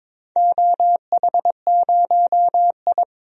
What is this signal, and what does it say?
Morse code, heard as a keyed tone, about 22 wpm, 700 Hz. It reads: OH0I
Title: DXCC-100 Call Signs - 4 Characters - Repeated 3x - 22wpm